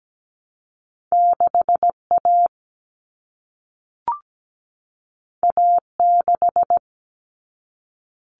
Morse code, heard as a keyed tone, about 17 wpm, 700 Hz. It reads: 6A A6
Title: All Cut Numbers - 2-Digit 17wpm